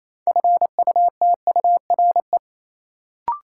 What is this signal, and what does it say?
Morse code, heard as a keyed tone, about 28 wpm, 700 Hz. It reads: FUTURE
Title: Sets of 3 Words - New Word Drill - 28wpm